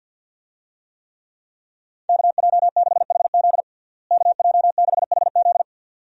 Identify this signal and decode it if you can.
Morse code, heard as a keyed tone, about 46 wpm, 700 Hz. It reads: KJ6HZ KJ6HZ
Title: USA Call Signs - 5 Characters - Repeated 2x - 50wpm